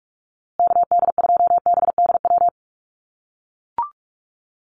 Morse code, heard as a keyed tone, about 44 wpm, 700 Hz. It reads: KD2BDW
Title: USA Call Signs Repeated 3x 45wpm